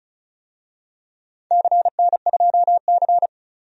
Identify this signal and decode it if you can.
Morse code, heard as a keyed tone, about 36 wpm, 700 Hz. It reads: CN2C
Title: DXCC-100 Call Signs - 4 Characters - 35wpm